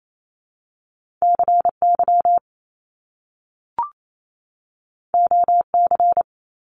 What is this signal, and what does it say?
Morse code, heard as a keyed tone, about 28 wpm, 700 Hz. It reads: CY OC
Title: Top Two Letter Combinations in Words 28wpm